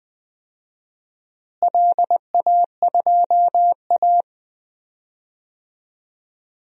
Morse code, heard as a keyed tone, about 20 wpm, 700 Hz. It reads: LA2A
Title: DXCC-100 Call Signs - 4 Characters - Repeated 2x - 20wpm